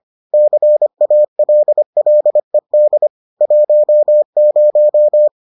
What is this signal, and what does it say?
Morse code, heard as a keyed tone, about 25 wpm, 600 Hz. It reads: CALLED 10